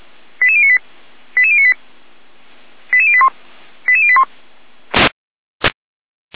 Five-tone selective call (ZVEI), heard as quick runs of five tones, tones of about 70 ms. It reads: 88098 88098 88081 88081